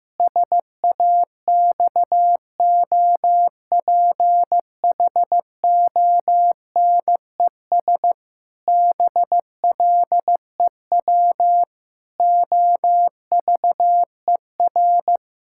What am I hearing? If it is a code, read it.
Morse code, heard as a keyed tone, about 15 wpm, 700 Hz. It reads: SAXOPHONES BLEW OVER
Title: Pangram Challenge v2 - 15wpm